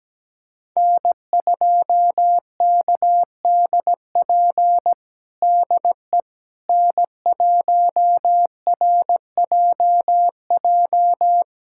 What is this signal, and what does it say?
Morse code, heard as a keyed tone, about 17 wpm, 700 Hz. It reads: N2KDP DE N1RJJ
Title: USA Callsign DE Callsign - 5 Characters - 17wpm